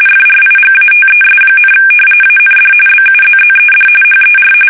RTTY, about 45.45 baud. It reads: FRIGATES, AND BUZZED HER
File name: fsk850 / battleshipnj